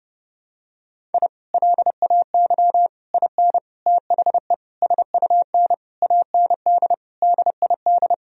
Morse code, heard as a keyed tone, about 30 wpm, 700 Hz. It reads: I LAY IN THE SUN AND DID
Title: Sentences from Top 1000 Words - Review Part 2 - 30wpm